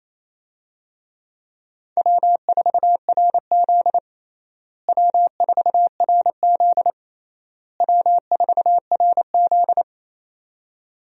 Morse code, heard as a keyed tone, about 28 wpm, 700 Hz. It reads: W4RZ W4RZ W4RZ
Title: USA Call Signs - 4 Characters - Repeated 3x - 28wpm